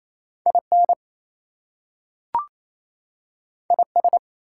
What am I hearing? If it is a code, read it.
Morse code, heard as a keyed tone, about 28 wpm, 700 Hz. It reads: IN IS
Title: Two Letter Words - 28wpm